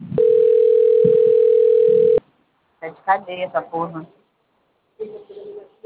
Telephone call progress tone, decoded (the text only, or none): ringback tone